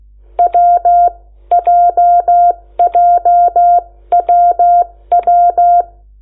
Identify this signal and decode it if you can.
Morse code, heard as a keyed tone, about 16 wpm, spaced as for 13 wpm, 700 Hz. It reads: WJJWW
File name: Lecciones / CLASE Nº8